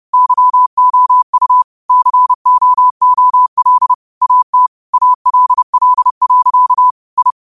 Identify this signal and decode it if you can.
Morse code, heard as a keyed tone, about 30 wpm, 1000 Hz. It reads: YOU COOL AT ALL. I